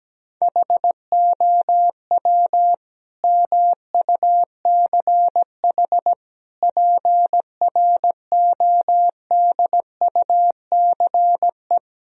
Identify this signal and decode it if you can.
Morse code, heard as a keyed tone, about 17 wpm, 700 Hz. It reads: HOW MUCH PRODUCE
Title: Sentences from Top 1000 Words - Review Part 3 - 17wpm